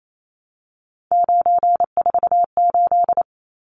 Morse code, heard as a keyed tone, about 28 wpm, 700 Hz. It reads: 948
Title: Numbers - 3-Digit 28wpm